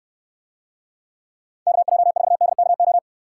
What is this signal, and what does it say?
Morse code, heard as a keyed tone, about 64 wpm, 700 Hz. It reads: KJ4DKO